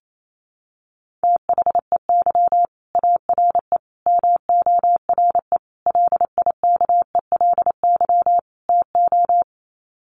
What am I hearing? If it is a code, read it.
Morse code, heard as a keyed tone, about 28 wpm, 700 Hz. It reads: THEY ARE MORE LIKELY TO
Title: Sets of 5 Words Encore2 28wpm